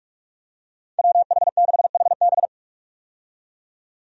Morse code, heard as a keyed tone, about 41 wpm, 700 Hz. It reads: WH6HB
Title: USA Call Signs - 5 Characters - 45wpm